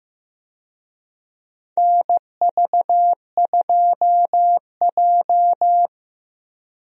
Morse code, heard as a keyed tone, about 16 wpm, 700 Hz. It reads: NV2J
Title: USA Call Signs - 4 Characters - Repeated 3x - 15wpm